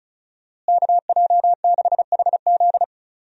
Morse code, heard as a keyed tone, about 36 wpm, 700 Hz. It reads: KJ6HZ
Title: USA Call Signs - 5 Characters - Repeated 2x - 35wpm